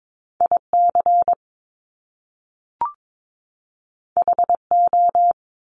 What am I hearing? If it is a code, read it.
Morse code, heard as a keyed tone, about 22 wpm, 700 Hz. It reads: IC HO
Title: Top Two Letter Combinations in Words 22wpm